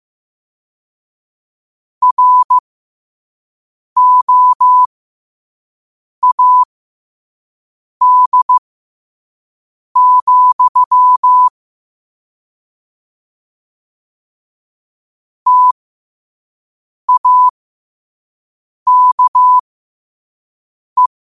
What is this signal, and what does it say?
Morse code, heard as a keyed tone, about 15 wpm, spaced as for 5 wpm, 1000 Hz. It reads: ROAD, TAKE
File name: qotdfiles / 5wpm